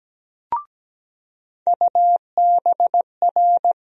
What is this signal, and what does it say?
Morse code, heard as a keyed tone, about 17 wpm, 700 Hz. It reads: UBR
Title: USA Call Sign Postfixes - 3 Letters 17wpm